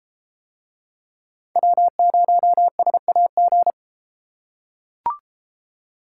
Morse code, heard as a keyed tone, about 31 wpm, 700 Hz. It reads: W0SAG